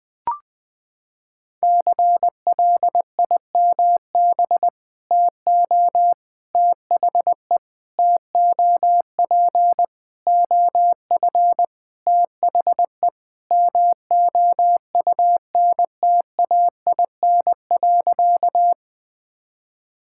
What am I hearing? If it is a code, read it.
Morse code, heard as a keyed tone, about 20 wpm, 700 Hz. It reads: CLIMB TO THE TOP OF THE MOUNTAIN.
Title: Sentences with 5-Word Phrases - 20wpm